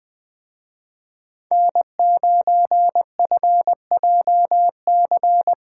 Morse code, heard as a keyed tone, about 20 wpm, 700 Hz. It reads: N9FJC
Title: Speed Racing - USA 1x3 Call Signs 17wpm